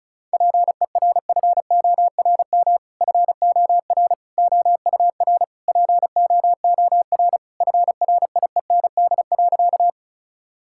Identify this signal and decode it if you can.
Morse code, heard as a keyed tone, about 36 wpm, 700 Hz. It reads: PERFORM FOR OUR POOR FRIEND.
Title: Sentences from Top 1000 Words - Review Part 5 - 35wpm